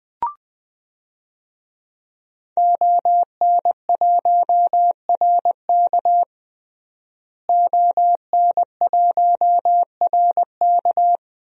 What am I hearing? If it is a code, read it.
Morse code, heard as a keyed tone, about 20 wpm, 700 Hz. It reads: ON1RK ON1RK
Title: DXCC-100 Call Signs - 5 Characters - Repeated 3x - 20wpm